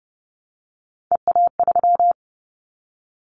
Morse code, heard as a keyed tone, about 30 wpm, 700 Hz. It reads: EA3
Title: Callsign Prefixes for Top 25 DXCC - Repeated 3x - 30wpm